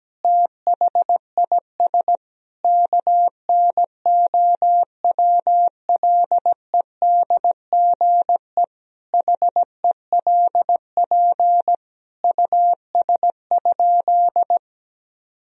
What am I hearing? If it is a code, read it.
Morse code, heard as a keyed tone, about 17 wpm, 700 Hz. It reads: THIS KNOWLEDGE HELP US?
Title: Sentences from Top 1000 Words - Review Part 3 - 17wpm